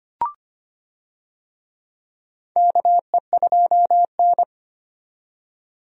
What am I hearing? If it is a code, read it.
Morse code, heard as a keyed tone, about 25 wpm, 700 Hz. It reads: KE2N
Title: USA Call Signs - 4 Characters - 25wpm